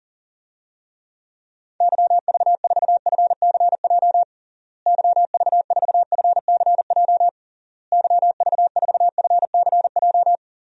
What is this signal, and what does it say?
Morse code, heard as a keyed tone, about 40 wpm, 700 Hz. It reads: YV4FCJ YV4FCJ YV4FCJ
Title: DXCC-100 Call Signs - 6 Characters - Repeated 3x - 40wpm